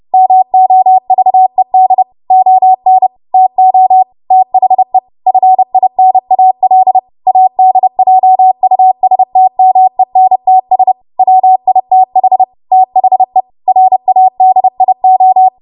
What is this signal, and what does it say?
Morse code, heard as a keyed tone, about 30 wpm, 750 Hz. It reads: MOVED ON TO THE FINAL ADJUSTMENTS WITH THE RADIO
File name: Archive / 30 WPM / 30WPM